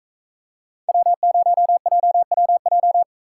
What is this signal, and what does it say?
Morse code, heard as a keyed tone, about 41 wpm, 700 Hz. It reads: W0JWJ